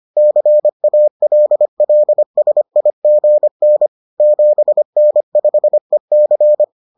Morse code, heard as a keyed tone, about 25 wpm, 600 Hz. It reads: CALLSIGN 7N5EC